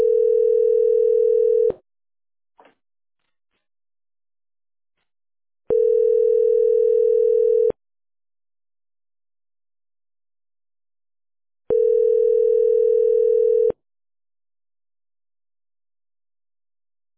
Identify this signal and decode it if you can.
Ringback tone (North America), 440 + 480 Hz.